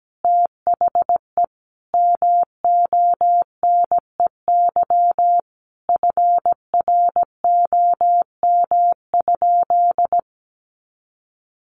Morse code, heard as a keyed tone, about 17 wpm, 700 Hz. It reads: THE MONEY FROM?
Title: Sentences from Top 400 Words 17wpm